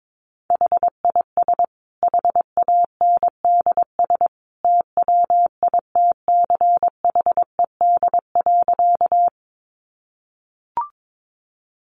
Morse code, heard as a keyed tone, about 22 wpm, 700 Hz. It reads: HIS HANDS TWITCHED.
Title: A Sound Of Thunder 22wpm